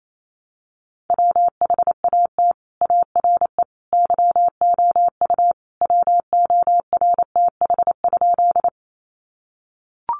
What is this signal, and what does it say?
Morse code, heard as a keyed tone, about 28 wpm, 700 Hz. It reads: WHAT ARE YOU WORTH?
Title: Sentences from Top 800 Words 28wpm